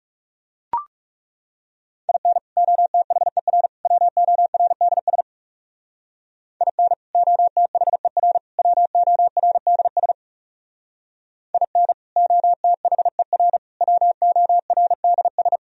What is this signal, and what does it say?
Morse code, heard as a keyed tone, about 38 wpm, 700 Hz. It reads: IN OTHER WORDS IN OTHER WORDS IN OTHER WORDS